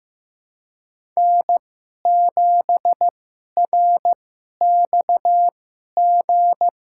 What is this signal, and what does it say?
Morse code, heard as a keyed tone, about 15 wpm, spaced as for 11 wpm, 700 Hz. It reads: N7RXG
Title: USA Call Signs - 5 Characters - 2x Character Spacing - 15wpm